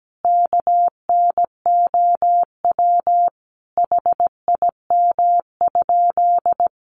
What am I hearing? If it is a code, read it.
Morse code, heard as a keyed tone, about 17 wpm, 700 Hz. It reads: KNOW HIM?
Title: Sentences from Top 100 Words 17wpm